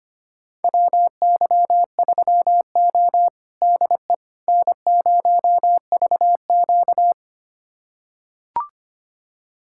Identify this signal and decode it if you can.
Morse code, heard as a keyed tone, about 25 wpm, 700 Hz. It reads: WY3O DE N0VQ
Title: USA Callsign DE Callsign - 4 Characters - 25wpm